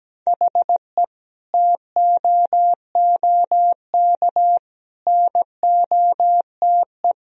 Morse code, heard as a keyed tone, about 17 wpm, 700 Hz. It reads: HE TOOK NOTE
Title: Sentences from Top 1000 Words - Review Part 2 - 17wpm